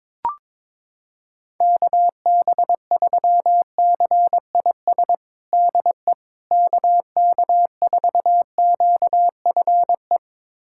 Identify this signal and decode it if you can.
Morse code, heard as a keyed tone, about 22 wpm, 700 Hz. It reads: KB3CIS DE KK4QFE
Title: USA Callsign DE Callsign - 6 Characters - 22wpm